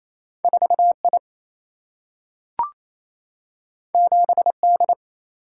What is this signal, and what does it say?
Morse code, heard as a keyed tone, about 28 wpm, 700 Hz. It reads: 4 I 7 D
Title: Single-Letter Single-Number 28wpm